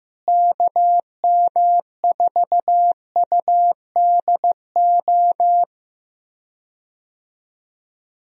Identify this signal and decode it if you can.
Morse code, heard as a keyed tone, about 15 wpm, 700 Hz. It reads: KM4UDO
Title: USA Call Signs - 6 Characters - Repeated 2x - 15wpm